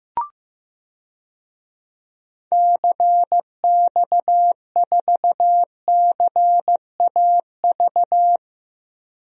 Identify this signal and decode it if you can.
Morse code, heard as a keyed tone, about 15 wpm, 700 Hz. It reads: CX4CAV
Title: DXCC-100 Call Signs - 6 Characters - Repeated 3x - 15wpm